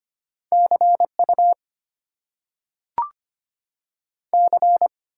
Morse code, heard as a keyed tone, about 25 wpm, 700 Hz. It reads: CU C
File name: Top 90 QSO Elements - 8x Character Spacing - 25wpm